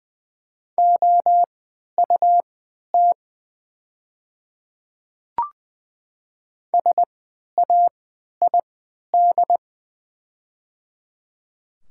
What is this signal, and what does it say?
Morse code, heard as a keyed tone, about 20 wpm, spaced as for 11 wpm, 700 Hz. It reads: OUT SAID
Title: 3x Character Spacing - Top 100 Words - 20wpm